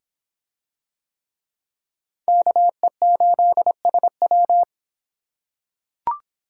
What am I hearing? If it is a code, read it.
Morse code, heard as a keyed tone, about 23 wpm, 700 Hz. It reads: KE8SW